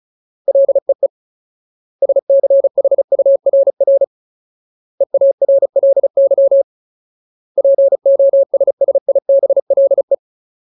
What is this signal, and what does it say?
Morse code, heard as a keyed tone, about 34 wpm, 550 Hz. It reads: LEE SCHURR EARLY POSSIBLE
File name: Ultimate Word List / 35WPM Ultimate